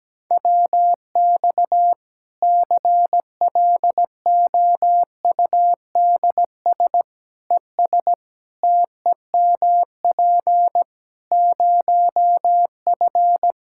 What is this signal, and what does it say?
Morse code, heard as a keyed tone, about 17 wpm, 700 Hz. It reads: WX CLOUDS ES TEMP 0F
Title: Practice QSO - Exchange 2 - Info - 17wpm